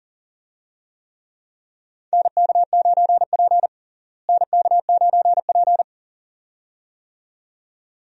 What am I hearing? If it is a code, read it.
Morse code, heard as a keyed tone, about 40 wpm, 700 Hz. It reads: NK9P NK9P
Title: USA Call Signs - 4 Characters - Repeated 2x - 40wpm